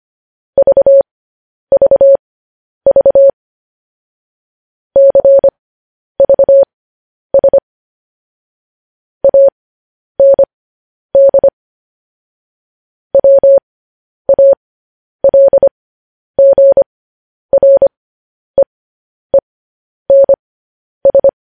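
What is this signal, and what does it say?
Morse code, heard as a keyed tone, about 25 wpm, spaced as for 10 wpm, 550 Hz. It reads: VVV CVS AND WALGREENS
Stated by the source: Morse code news headlines practice at 10 wpm.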